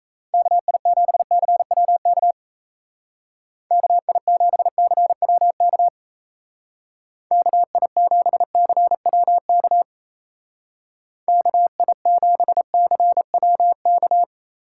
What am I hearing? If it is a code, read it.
Morse code, heard as a keyed tone, about 37 wpm, 700 Hz. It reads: KI7CWK KI7CWK KI7CWK KI7CWK